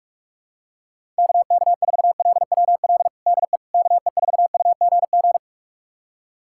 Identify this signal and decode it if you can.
Morse code, heard as a keyed tone, about 46 wpm, 700 Hz. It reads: KK4LWL DE KE4UGG
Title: USA Callsign DE Callsign - 6 Characters - 45wpm